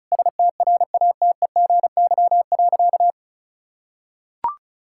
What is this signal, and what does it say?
Morse code, heard as a keyed tone, about 36 wpm, 700 Hz. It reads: STRATEGY.
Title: Sentences from Top 1000 Words - Review Part 5 - 35wpm